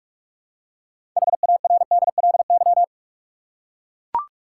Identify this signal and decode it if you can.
Morse code, heard as a keyed tone, about 41 wpm, 700 Hz. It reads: HARDLY
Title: Six Letter Words - 45wpm